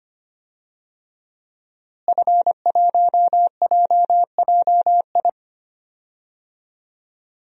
Morse code, heard as a keyed tone, about 25 wpm, 700 Hz. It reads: F1JJI
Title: DXCC-100 Call Signs - 5 Characters - Repeated 3x - 25wpm